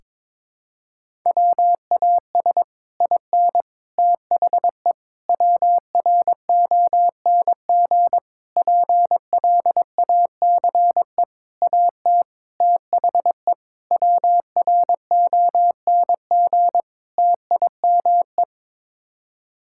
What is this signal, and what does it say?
Morse code, heard as a keyed tone, about 22 wpm, 700 Hz. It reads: WAS IN THE WRONG PLACE AT THE WRONG TIME
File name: Sets of 9 Words - 22wpm